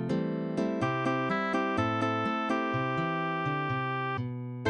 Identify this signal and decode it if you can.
There is no code.